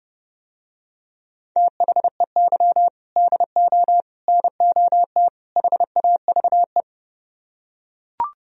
Morse code, heard as a keyed tone, about 30 wpm, 700 Hz. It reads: THEY DO NOT HAVE